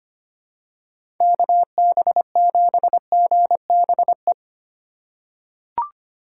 Morse code, heard as a keyed tone, about 25 wpm, 700 Hz. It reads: KB7GBE
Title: USA Call Signs - 6 Characters - 4x Character Spacing - 25wpm